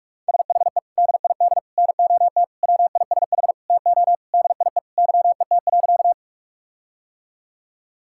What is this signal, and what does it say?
Morse code, heard as a keyed tone, about 43 wpm, 700 Hz. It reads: SHE DID NOT WISH TO DIE YET.
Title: Sentences from Top 1600 Words 45wpm